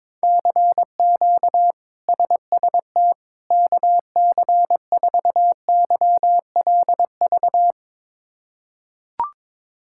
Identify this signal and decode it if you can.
Morse code, heard as a keyed tone, about 22 wpm, 700 Hz. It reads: CQ SST KC4YLV
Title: K1USN SST QSO Practice - 22wpm